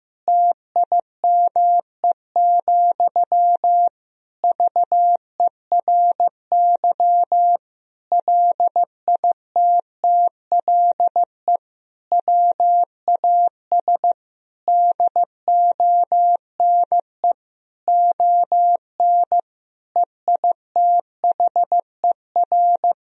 Morse code, heard as a keyed tone, about 15 wpm, 700 Hz. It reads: TIME, VERY LITTLE WAS DONE ON EITHER
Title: Sentences from Top 1000 Words - Review Part 4 - 15wpm